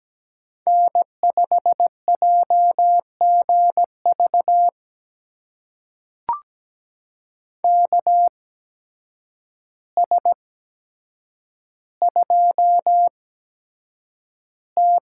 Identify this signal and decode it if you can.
Morse code, heard as a keyed tone, about 17 wpm, spaced as for 8 wpm, 700 Hz. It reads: N5JGV KS2T
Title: USA Call Signs - 8x Character Spacing - 17wpm